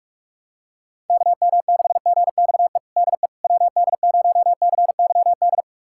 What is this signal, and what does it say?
Morse code, heard as a keyed tone, about 47 wpm, 700 Hz. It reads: KM6GXE DE WD0CYD
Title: USA Callsign DE Callsign - 6 Characters - 45wpm